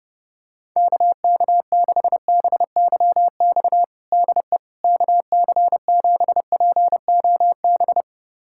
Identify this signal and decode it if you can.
Morse code, heard as a keyed tone, about 32 wpm, 700 Hz. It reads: KK6BYX DE KC7POB
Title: USA Callsign DE Callsign - 6 Characters - 30wpm